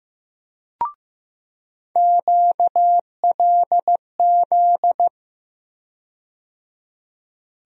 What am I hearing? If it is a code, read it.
Morse code, heard as a keyed tone, about 15 wpm, 700 Hz. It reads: QLZ
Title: USA Call Sign Postfixes - 3 Letters 15wpm